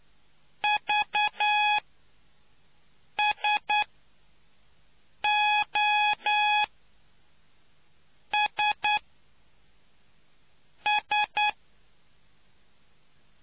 Morse code, heard as a keyed tone, about 10 wpm, spaced as for 4 wpm, 840 Hz. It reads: VSOSS